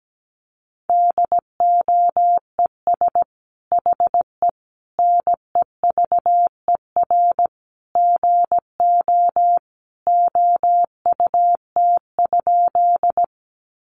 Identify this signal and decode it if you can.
Morse code, heard as a keyed tone, about 17 wpm, 700 Hz. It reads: DOES HE NEVER GO OUT?
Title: Sentences from Top 300 Words 17wpm